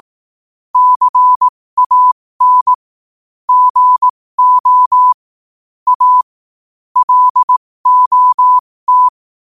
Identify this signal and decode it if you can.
Morse code, heard as a keyed tone, about 18 wpm, spaced as for 15 wpm, 1000 Hz. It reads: CAN GO A LOT
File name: qotdfiles / 15wpm